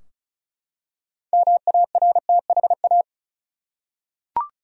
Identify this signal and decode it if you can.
Morse code, heard as a keyed tone, about 36 wpm, 700 Hz. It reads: MARTHA
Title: USA Common Names - 35wpm